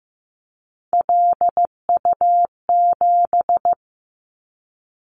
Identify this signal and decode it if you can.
Morse code, heard as a keyed tone, about 15 wpm, 700 Hz. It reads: LU7
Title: Callsign Prefixes for Top 50 DXCC - 15wpm